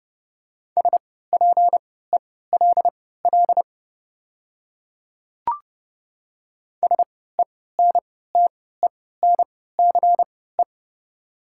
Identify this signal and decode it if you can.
Morse code, heard as a keyed tone, about 30 wpm, spaced as for 17 wpm, 700 Hz. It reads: SPELL SENTENCE
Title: 3x Character Spacing - Top 200 Words - 30wpm